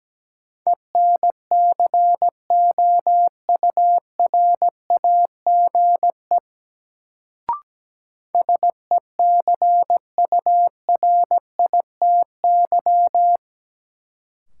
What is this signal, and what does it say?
Morse code, heard as a keyed tone, about 17 wpm, 700 Hz. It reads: ENCOURAGE SECURITY
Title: Sentences from Top 1400 Words - New Word Drill Alt - 17wpm